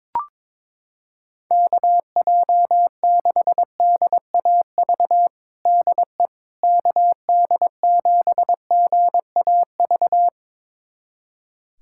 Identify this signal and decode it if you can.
Morse code, heard as a keyed tone, about 21 wpm, 700 Hz. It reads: KJ6DAV DE KD7GAV
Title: USA Callsign DE Callsign - 6 Characters - 22wpm